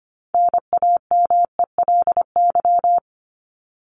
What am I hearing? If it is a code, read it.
Morse code, heard as a keyed tone, about 25 wpm, 700 Hz. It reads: NAMELY
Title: LY Suffix - Top 100 Words with Base Words 25wpm